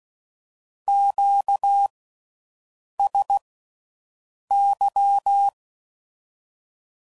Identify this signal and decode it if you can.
Morse code, heard as a keyed tone, about 16 wpm, spaced as for 6 wpm, 800 Hz. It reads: QSY